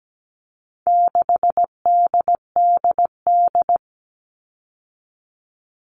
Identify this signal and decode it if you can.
Morse code, heard as a keyed tone, about 17 wpm, 700 Hz. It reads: 6DDD
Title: All Cut Numbers - 4-Digit 17wpm